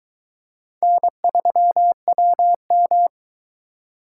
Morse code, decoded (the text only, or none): N3WM